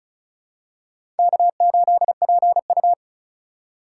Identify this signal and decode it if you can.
Morse code, heard as a keyed tone, about 36 wpm, 700 Hz. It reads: K8PU
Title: USA Call Signs - 4 Characters - 4x Character Spacing - 35wpm